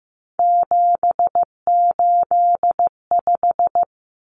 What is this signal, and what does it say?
Morse code, heard as a keyed tone, about 15 wpm, 700 Hz. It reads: 785
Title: Numbers - 3-Digit 15wpm